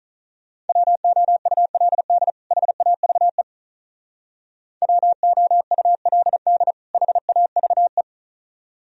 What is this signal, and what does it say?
Morse code, heard as a keyed tone, about 43 wpm, 700 Hz. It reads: WOULD HAVE WOULD HAVE
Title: Speed-Racing - Sets of 2 Words 35wpm